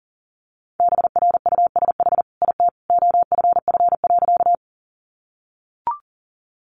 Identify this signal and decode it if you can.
Morse code, heard as a keyed tone, about 37 wpm, 700 Hz. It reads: BRUSH IT OFF.
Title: Sentences from Top 1100 Words 40wpm